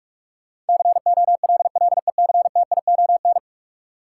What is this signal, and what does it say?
Morse code, heard as a keyed tone, about 41 wpm, 700 Hz. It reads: COLLECTION
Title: Sentences from Top 1200 Words - New Word Drill Alt - 45wpm